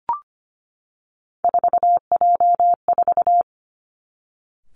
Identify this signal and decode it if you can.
Morse code, heard as a keyed tone, about 25 wpm, 700 Hz. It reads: 4J4
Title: Three Challenging Letter-Number Combinations 25wpm